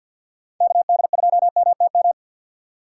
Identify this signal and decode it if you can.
Morse code, heard as a keyed tone, about 53 wpm, 700 Hz. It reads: KD2MTM
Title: USA Call Signs - 6 Characters - 3x Character Spacing - 50wpm